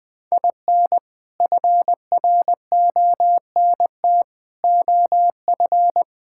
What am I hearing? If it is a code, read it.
Morse code, heard as a keyed tone, about 20 wpm, 700 Hz. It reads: IN FRONT OF
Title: Build A Phrase - 4-Word Phrases - 20wpm